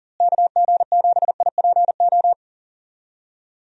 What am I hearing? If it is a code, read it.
Morse code, heard as a keyed tone, about 40 wpm, 700 Hz. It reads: KG7IPO
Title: USA Call Signs - 6 Characters - Repeated 2x - 40wpm